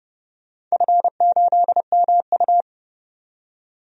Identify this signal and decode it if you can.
Morse code, heard as a keyed tone, about 30 wpm, 700 Hz. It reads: F8MU
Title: DXCC-100 Call Signs - 4 Characters - 30wpm